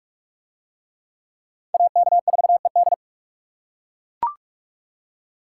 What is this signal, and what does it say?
Morse code, heard as a keyed tone, about 42 wpm, 700 Hz. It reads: AK4ED